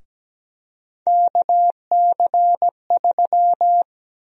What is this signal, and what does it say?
Morse code, heard as a keyed tone, about 17 wpm, 700 Hz. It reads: KC3
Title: USA Call Sign Prefixes - Repeated 2x - 17wpm